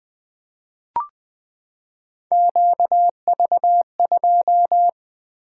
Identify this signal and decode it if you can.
Morse code, heard as a keyed tone, about 20 wpm, 700 Hz. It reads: QV2
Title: Three Letter-Number Combinations - 20wpm